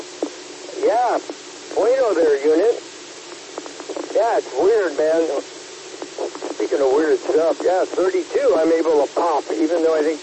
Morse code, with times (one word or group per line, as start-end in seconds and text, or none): none